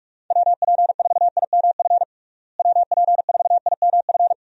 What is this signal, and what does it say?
Morse code, heard as a keyed tone, about 42 wpm, 700 Hz. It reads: WP4IMF WP4IMF
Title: USA Call Signs - 6 Characters - Repeated 2x - 45wpm